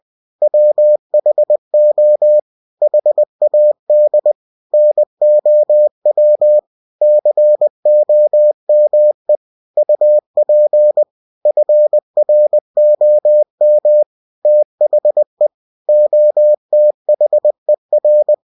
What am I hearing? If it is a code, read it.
Morse code, heard as a keyed tone, about 20 wpm, 600 Hz. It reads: WHO HAD NOW COME UP FROM THE OTHER
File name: cw-books / winnie-the-pooh / winnie-20wpm-page-0066